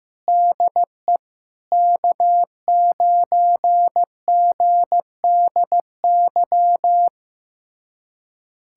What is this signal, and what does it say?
Morse code, heard as a keyed tone, about 15 wpm, 700 Hz. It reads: DE K9GDY
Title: USA Callsign DE Callsign 15wpm